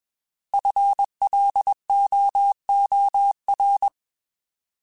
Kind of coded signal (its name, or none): Morse code